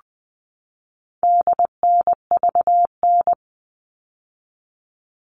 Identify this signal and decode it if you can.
Morse code, heard as a keyed tone, about 20 wpm, 700 Hz. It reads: DNVN
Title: All Cut Numbers - 4-Digit 20wpm